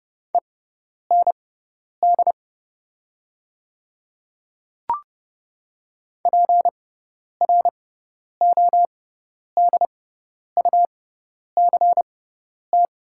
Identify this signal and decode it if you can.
Morse code, heard as a keyed tone, about 30 wpm, spaced as for 10 wpm, 700 Hz. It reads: END PRODUCT
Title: 6x Character Spacing - Top 500 Words - 30wpm